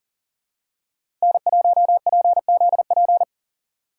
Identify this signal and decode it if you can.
Morse code, heard as a keyed tone, about 40 wpm, 700 Hz. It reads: N1PZP
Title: USA Call Signs - 5 Characters - 4x Character Spacing - 40wpm